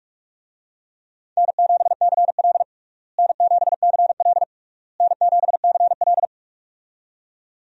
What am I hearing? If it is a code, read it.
Morse code, heard as a keyed tone, about 46 wpm, 700 Hz. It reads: N7CL N7CL N7CL
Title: USA Call Signs - 4 Characters - Repeated 3x - 45wpm